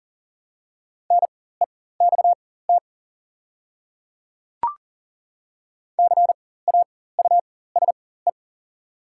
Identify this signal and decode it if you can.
Morse code, heard as a keyed tone, about 40 wpm, spaced as for 19 wpm, 700 Hz. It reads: NEXT CAUSE
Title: 4x Character Spacing - Top 500 Words - 40wpm